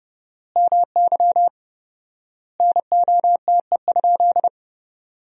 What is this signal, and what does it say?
Morse code, heard as a keyed tone, about 30 wpm, 700 Hz. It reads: MY NOTE?
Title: 4x Word-Spacing - Sentences from Top 500 Words - 30wpm